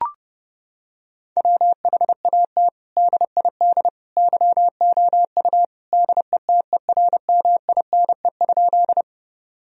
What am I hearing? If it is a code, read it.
Morse code, heard as a keyed tone, about 30 wpm, 700 Hz. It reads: WHAT DID YOU DETERMINE?
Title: Sentences from Top 1000 Words - Review Part 4 - 30wpm